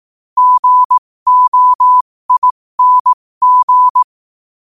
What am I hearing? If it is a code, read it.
Morse code, heard as a keyed tone, about 18 wpm, spaced as for 15 wpm, 1000 Hz. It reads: GOING
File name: qotdfiles / 15wpm